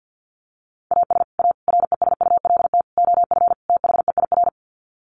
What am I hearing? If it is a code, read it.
Morse code, heard as a keyed tone, about 50 wpm, 700 Hz. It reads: AS A RESULT OF THEIR
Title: Sets of 5 Words Encore2 50wpm